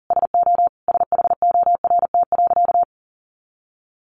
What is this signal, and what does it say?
Morse code, heard as a keyed tone, about 40 wpm, 700 Hz. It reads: SO SHORT.
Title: Sentences from Top 500 Words 40wpm